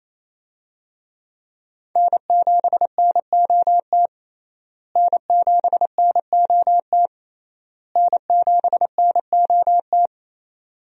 Morse code, heard as a keyed tone, about 28 wpm, 700 Hz. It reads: N7NOT N7NOT N7NOT
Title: USA Call Signs - 5 Characters - Repeated 3x - 28wpm